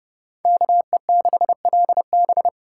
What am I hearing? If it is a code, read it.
Morse code, heard as a keyed tone, about 30 wpm, 700 Hz. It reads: KE6LB
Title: USA Callsign DE Callsign - 5 Characters - 30wpm